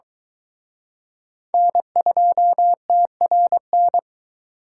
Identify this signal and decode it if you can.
Morse code, heard as a keyed tone, about 24 wpm, 700 Hz. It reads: N2TRN